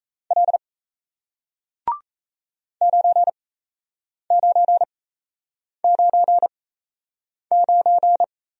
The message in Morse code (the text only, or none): L 9999